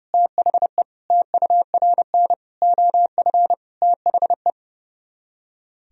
Morse code, heard as a keyed tone, about 40 wpm, spaced as for 35 wpm, 700 Hz. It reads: THE TURN OF THE
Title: Speed-Racing - Sets of 5 Words 30wpm